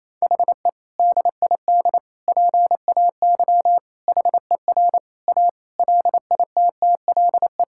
Morse code, heard as a keyed tone, about 28 wpm, 700 Hz. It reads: HE DID PAY HER A LITTLE
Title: Sentences from Top 1000 Words - Review Part 6 - 28wpm